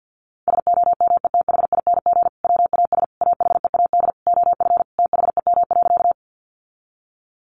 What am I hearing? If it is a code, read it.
Morse code, heard as a keyed tone, about 50 wpm, 700 Hz. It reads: SOMETHING WAS AHEAD OF THEM.
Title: Sentences from Top 1100 Words 50wpm